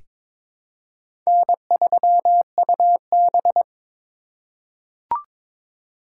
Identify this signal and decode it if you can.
Morse code, heard as a keyed tone, about 22 wpm, 700 Hz. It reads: N3UB
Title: USA Call Signs - 4 Characters - 4x Character Spacing - 22wpm